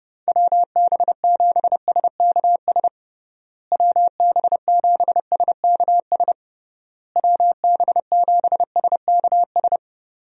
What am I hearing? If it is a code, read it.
Morse code, heard as a keyed tone, about 30 wpm, 700 Hz. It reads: WB7SKS WB7SKS WB7SKS
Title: USA Call Signs - 6 Characters - Repeated 3x - 30wpm